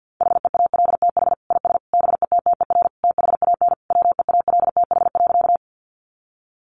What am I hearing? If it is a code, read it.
Morse code, heard as a keyed tone, about 48 wpm, 700 Hz. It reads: HEALTH IS BETTER THAN WEALTH.
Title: Sentences from Top 1200 Words 50wpm